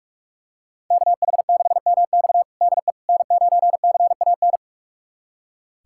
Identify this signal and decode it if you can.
Morse code, heard as a keyed tone, about 46 wpm, 700 Hz. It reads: KH6MX DE N9CAN
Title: USA Callsign DE Callsign - 5 Characters - 45wpm